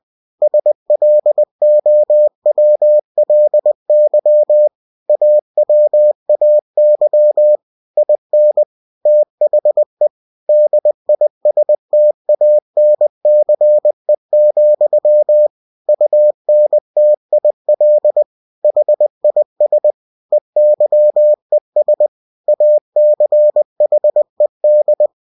Morse code, heard as a keyed tone, about 20 wpm, 600 Hz. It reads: SLOWLY AWAY IN THE DISTANCE, UNTIL HIS EYES ACHED
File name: cw-books / winnie-the-pooh / winnie-20wpm-page-0092